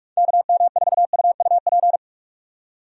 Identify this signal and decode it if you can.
Morse code, heard as a keyed tone, about 46 wpm, 700 Hz. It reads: KM4UUP
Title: USA Call Signs - 6 Characters - 45wpm